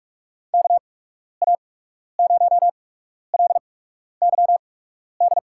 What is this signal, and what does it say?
Morse code, heard as a keyed tone, about 45 wpm, spaced as for 12 wpm, 700 Hz. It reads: KA0LYD
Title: USA Call Signs - 6 Characters - 8x Character Spacing - 45wpm